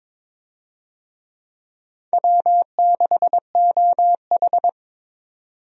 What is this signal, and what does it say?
Morse code, heard as a keyed tone, about 22 wpm, 700 Hz. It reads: W6OH
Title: USA Call Signs - 4 Characters - 22wpm